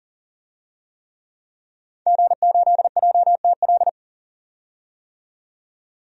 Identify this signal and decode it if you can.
Morse code, heard as a keyed tone, about 40 wpm, 700 Hz. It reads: G8JTL
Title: DXCC-100 Call Signs - 5 Characters - 40wpm